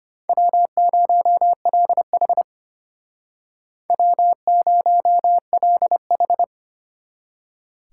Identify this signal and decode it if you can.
Morse code, heard as a keyed tone, about 33 wpm, 700 Hz. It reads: W0LH W0LH